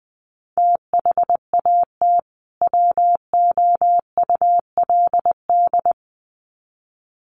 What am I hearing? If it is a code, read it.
Morse code, heard as a keyed tone, about 20 wpm, 700 Hz. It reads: THAT WOULD
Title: Sets of 2 Words 20wpm